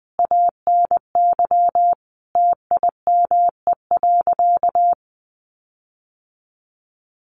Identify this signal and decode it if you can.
Morse code, heard as a keyed tone, about 20 wpm, 700 Hz. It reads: ANY TIME.
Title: Sentences from Top 1500 Words 20wpm